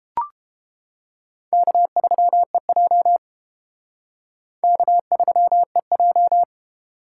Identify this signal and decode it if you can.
Morse code, heard as a keyed tone, about 29 wpm, 700 Hz. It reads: K3EJ K3EJ